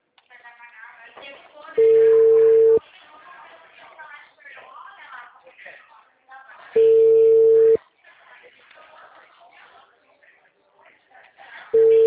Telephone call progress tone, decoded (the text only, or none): ringback tone